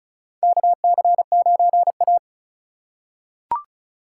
Morse code, heard as a keyed tone, about 36 wpm, 700 Hz. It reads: KC9A
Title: USA Call Signs - 3x Character Spacing - 35wpm